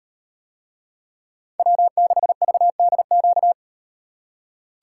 Morse code, heard as a keyed tone, about 38 wpm, 700 Hz. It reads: W6VDQ